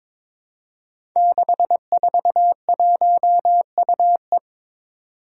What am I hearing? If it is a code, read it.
Morse code, heard as a keyed tone, about 22 wpm, 700 Hz. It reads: 641UE
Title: Five Challenging Letter-Number Combinations 22wpm